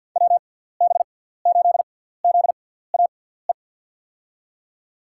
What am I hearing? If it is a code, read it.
Morse code, heard as a keyed tone, about 47 wpm, spaced as for 17 wpm, 700 Hz. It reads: WB8ZAE
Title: USA Call Signs - 6 Characters - 6x Character Spacing - 50wpm